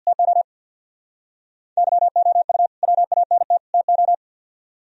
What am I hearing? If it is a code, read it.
Morse code, heard as a keyed tone, about 62 wpm, 700 Hz. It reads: TO YOU WANT TO
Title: Speed-Racing - Sets of 3 Words 50wpm